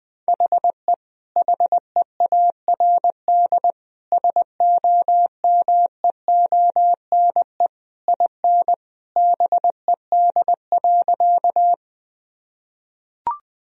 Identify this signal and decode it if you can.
Morse code, heard as a keyed tone, about 20 wpm, 700 Hz. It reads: HE HEARD SOMEONE IN BED.
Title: Sentences from Top 1000 Words - Review Part 4 - 20wpm